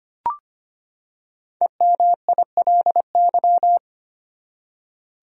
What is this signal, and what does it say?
Morse code, heard as a keyed tone, about 25 wpm, 700 Hz. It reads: EMILY
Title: USA Common Names - 25wpm